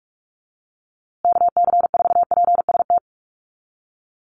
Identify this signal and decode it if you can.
Morse code, heard as a keyed tone, about 44 wpm, 700 Hz. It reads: KC4PST
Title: USA Call Signs Repeated 3x 45wpm